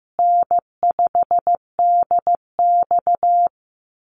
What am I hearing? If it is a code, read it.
Morse code, heard as a keyed tone, about 15 wpm, 700 Hz. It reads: N5DX
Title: Easy Call Signs Repeated 3x - 15wpm